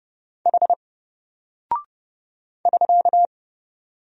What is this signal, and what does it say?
Morse code, heard as a keyed tone, about 30 wpm, 700 Hz. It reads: H <SK>
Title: Single Character (Letter:Number:Punctuation:Prosign) Alt 30wpm